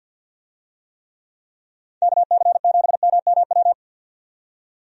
Morse code, heard as a keyed tone, about 51 wpm, 700 Hz. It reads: KC7MMW
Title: USA Call Signs - 6 Characters - 50wpm